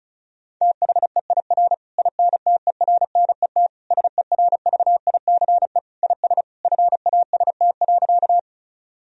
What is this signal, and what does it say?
Morse code, heard as a keyed tone, about 36 wpm, 700 Hz. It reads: THEIR INTERNET SERVICE IS FAST.
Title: Sentences from Top 1000 Words - Review Part 4 - 35wpm